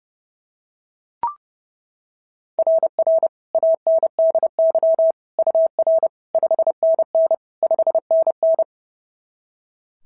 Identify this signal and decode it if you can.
Morse code, heard as a keyed tone, about 26 wpm, 650 Hz. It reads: RR ANDY UR 5NN 5NN
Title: SOTA QSO Practice 30wpm